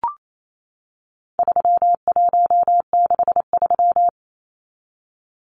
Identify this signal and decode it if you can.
Morse code, heard as a keyed tone, about 28 wpm, 700 Hz. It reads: 3163
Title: Numbers - 4-Digit 28wpm